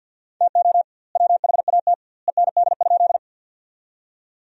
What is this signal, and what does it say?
Morse code, heard as a keyed tone, about 50 wpm, 700 Hz. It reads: TO WHAT END?
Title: 2x Word-Spacing - Sentences from Top 200 Words - 50wpm